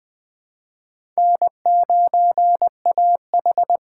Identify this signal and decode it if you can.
Morse code, heard as a keyed tone, about 20 wpm, 700 Hz. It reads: N9AH
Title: USA Call Signs - 4 Characters - 4x Character Spacing - 20wpm